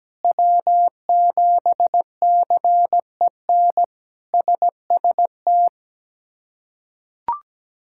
Morse code, heard as a keyed tone, about 17 wpm, 700 Hz. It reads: W7CEN SST
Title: K1USN SST QSO Practice - 17wpm